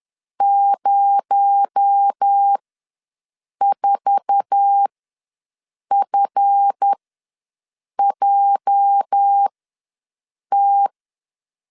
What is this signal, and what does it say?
Morse code, heard as a keyed tone, about 10 wpm, spaced as for 6 wpm, 800 Hz. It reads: 04FJT
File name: LEZIONI / 10 SIGNS / 10C DOT ws 10wpm 4213